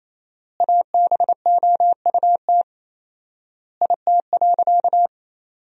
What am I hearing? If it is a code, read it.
Morse code, heard as a keyed tone, about 28 wpm, 700 Hz. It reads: ABOUT IT.
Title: 4x Word-Spacing - Sentences from Top 400 Words - 28wpm